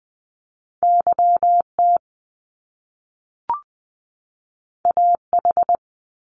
Morse code, heard as a keyed tone, about 20 wpm, 700 Hz. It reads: YT AH
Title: Top Two Letter Combinations in Words 20wpm